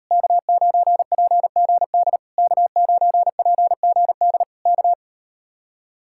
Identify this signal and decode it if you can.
Morse code, heard as a keyed tone, about 38 wpm, 700 Hz. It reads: K9PGD K9PGD K